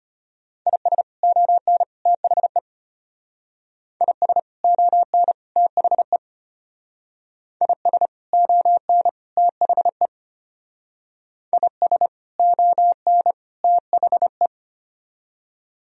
Morse code, heard as a keyed tone, about 32 wpm, 700 Hz. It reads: IS ON THE IS ON THE IS ON THE IS ON THE